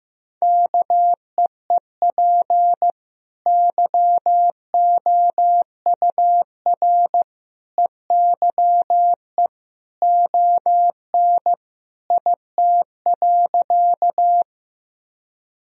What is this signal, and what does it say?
Morse code, heard as a keyed tone, about 15 wpm, 700 Hz. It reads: KEEP YOUR EYE ON IT.
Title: Sentences from Top 1000 Words - Review Part 1 - 15wpm